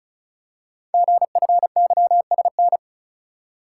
Morse code, heard as a keyed tone, about 36 wpm, 700 Hz. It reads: GFYSN
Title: Code Groups - 5 Letter:Numbers - 35wpm